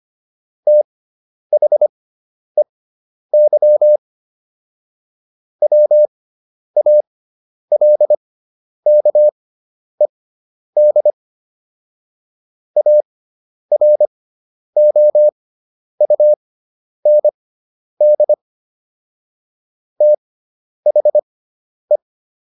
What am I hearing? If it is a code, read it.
Morse code, heard as a keyed tone, about 25 wpm, spaced as for 10 wpm, 600 Hz. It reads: THEY WALKED AROUND THE